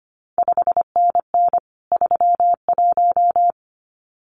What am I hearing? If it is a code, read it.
Morse code, heard as a keyed tone, about 25 wpm, 700 Hz. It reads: 5NN 31
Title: CQ WW DX Contest Exchange - 25wpm